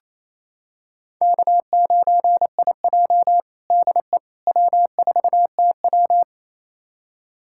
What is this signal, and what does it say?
Morse code, heard as a keyed tone, about 28 wpm, 700 Hz. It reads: K9IJ DE W4TW
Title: USA Callsign DE Callsign - 4 Characters - 28wpm